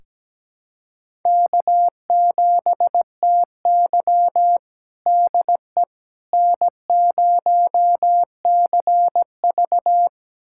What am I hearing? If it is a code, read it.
Morse code, heard as a keyed tone, about 17 wpm, 700 Hz. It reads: K7TY DE N0CV
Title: USA Callsign DE Callsign - 4 Characters - 17wpm